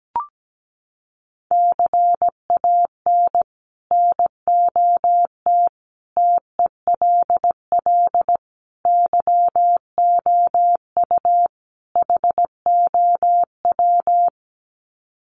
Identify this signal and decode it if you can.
Morse code, heard as a keyed tone, about 17 wpm, 700 Hz. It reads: CAN NOT TELL YOU HOW
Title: Sets of 5 Words Encore2 17wpm